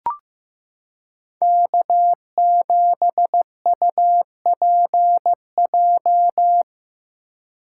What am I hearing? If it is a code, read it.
Morse code, heard as a keyed tone, about 15 wpm, 700 Hz. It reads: K7UPJ
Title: K1USN SST QSO Practice - 15wpm